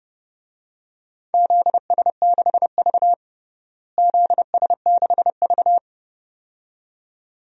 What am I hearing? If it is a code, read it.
Morse code, heard as a keyed tone, about 29 wpm, 700 Hz. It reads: ZS6V ZS6V
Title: DXCC-100 Call Signs - 4 Characters - Repeated 2x - 30wpm